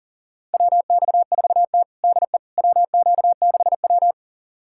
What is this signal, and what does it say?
Morse code, heard as a keyed tone, about 37 wpm, 700 Hz. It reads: WX4T DE WQ6W
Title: USA Callsign DE Callsign - 4 Characters - 40wpm